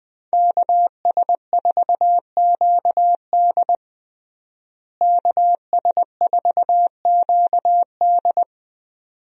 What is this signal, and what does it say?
Morse code, heard as a keyed tone, about 20 wpm, 700 Hz. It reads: KS4QD KS4QD
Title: USA Call Signs - 5 Characters - Repeated 2x - 20wpm